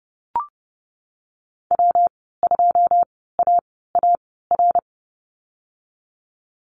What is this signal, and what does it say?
Morse code, heard as a keyed tone, about 30 wpm, spaced as for 17 wpm, 700 Hz. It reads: W2AAR
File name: Lesson 19g - Letter w - Call Signs 30wpm